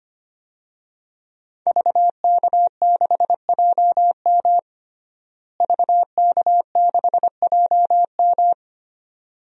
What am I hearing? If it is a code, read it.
Morse code, heard as a keyed tone, about 25 wpm, 700 Hz. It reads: VK6JM VK6JM
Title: DXCC-100 Call Signs - 5 Characters - Repeated 2x - 25wpm